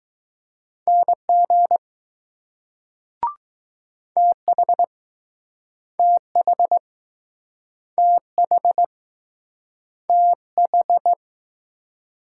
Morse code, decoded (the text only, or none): NG TH TH TH TH